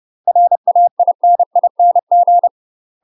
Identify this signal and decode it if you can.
Morse code, heard as a keyed tone, about 30 wpm, 700 Hz. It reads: RAINING
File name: UK MASTER QSOs TRAINING / 30WPM / UK MASTER QSOs 0031